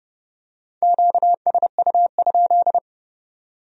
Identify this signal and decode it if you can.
Morse code, heard as a keyed tone, about 27 wpm, 700 Hz. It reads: QSU?